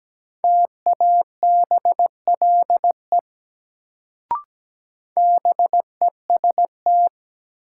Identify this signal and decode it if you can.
Morse code, heard as a keyed tone, about 17 wpm, 700 Hz. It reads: TABLE BEST
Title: Sentences from Top 400 Words - New Word Drill - 17wpm